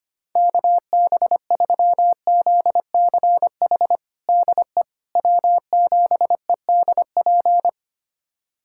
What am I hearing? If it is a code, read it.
Morse code, heard as a keyed tone, about 25 wpm, 700 Hz. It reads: KB3ZCH DE W7EDP
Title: USA Callsign DE Callsign 25wpm